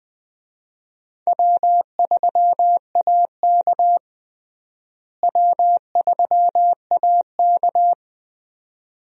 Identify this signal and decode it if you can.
Morse code, heard as a keyed tone, about 20 wpm, 700 Hz. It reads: W3AK W3AK
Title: USA Call Signs - 4 Characters - Repeated 2x - 20wpm